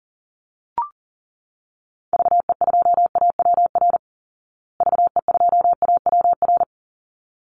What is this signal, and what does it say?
Morse code, heard as a keyed tone, about 40 wpm, 700 Hz. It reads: VE2AWR VE2AWR
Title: Call Signs Repeated 3x - 40wpm